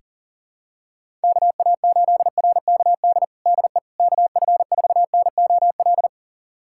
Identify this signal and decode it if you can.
Morse code, heard as a keyed tone, about 37 wpm, 700 Hz. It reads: KA8RKD DE KF4NOL
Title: USA Callsign DE Callsign - 6 Characters - 40wpm